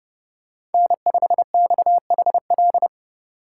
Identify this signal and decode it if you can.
Morse code, heard as a keyed tone, about 29 wpm, 700 Hz. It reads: N5XHL
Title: USA Call Signs - 4x Character Spacing - 30wpm